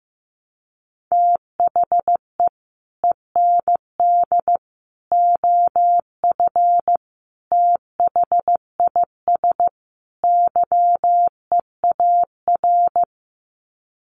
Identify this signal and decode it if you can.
Morse code, heard as a keyed tone, about 15 wpm, 700 Hz. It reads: THE END OF THIS YEAR
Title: Sets of 5 Words Encore 15wpm